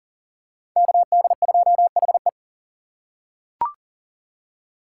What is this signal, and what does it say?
Morse code, heard as a keyed tone, about 40 wpm, 700 Hz. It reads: KD2HE
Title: K1USN SST QSO Practice - 40wpm